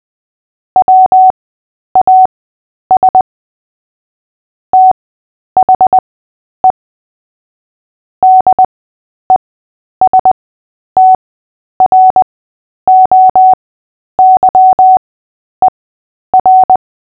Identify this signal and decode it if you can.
Morse code, heard as a keyed tone, about 20 wpm, spaced as for 10 wpm, 750 Hz. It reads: WAS THE DESTROYER
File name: cw / fists / Short-Story-3-10wpm